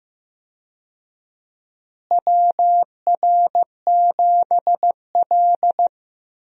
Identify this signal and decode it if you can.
Morse code, heard as a keyed tone, about 15 wpm, 700 Hz. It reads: WR7L
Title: USA Call Signs - 4 Characters - Repeated 3x - 15wpm